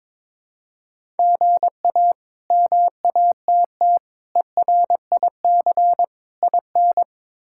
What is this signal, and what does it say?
Morse code, heard as a keyed tone, about 22 wpm, 700 Hz. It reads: GA MATT ERIC IN
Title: K1USN SST QSO Practice - 22wpm